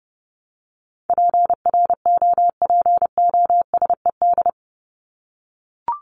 Morse code, heard as a keyed tone, about 30 wpm, 700 Hz. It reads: PROPOSED
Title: ED Suffix - Top 100 Words with Base Words 30wpm